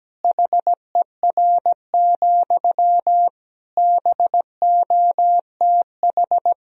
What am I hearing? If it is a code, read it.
Morse code, heard as a keyed tone, about 17 wpm, 700 Hz. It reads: HER, BOTH
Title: Sentences from Top 1000 Words - Review Part 4 - 17wpm